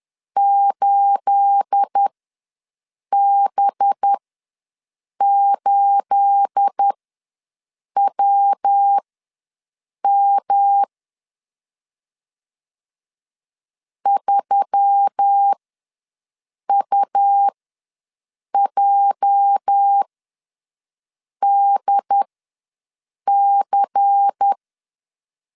Morse code, heard as a keyed tone, about 10 wpm, spaced as for 6 wpm, 800 Hz. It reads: 8B8WM 3UJDC
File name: LEZIONI / 9 28 ws 10wpm 4124